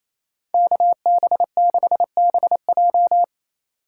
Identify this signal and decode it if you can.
Morse code, heard as a keyed tone, about 28 wpm, 700 Hz. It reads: KB6BJ
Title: USA Call Signs - 5 Characters - 4x Character Spacing - 28wpm